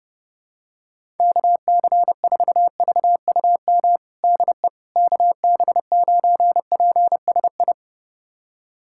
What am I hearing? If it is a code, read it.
Morse code, heard as a keyed tone, about 30 wpm, 700 Hz. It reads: KC4VUM DE KB9PSI
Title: USA Callsign DE Callsign - 6 Characters - 30wpm